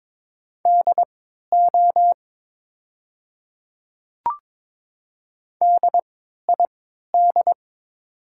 Morse code, heard as a keyed tone, about 22 wpm, spaced as for 12 wpm, 700 Hz. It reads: DO DID
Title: 3x Character Spacing - Top 100 Words - 22wpm